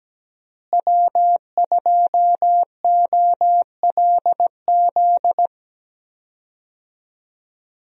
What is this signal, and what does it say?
Morse code, heard as a keyed tone, about 17 wpm, 700 Hz. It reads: W2OLZ
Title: USA Call Signs - 5 Characters - Repeated 3x - 17wpm